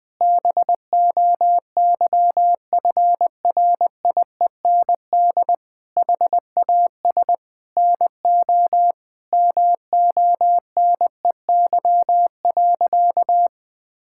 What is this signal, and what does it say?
Morse code, heard as a keyed tone, about 20 wpm, 700 Hz. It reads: BOYFRIEND HAS NO MONEY.
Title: Sentences from Top 1000 Words - Review Part 5 - 20wpm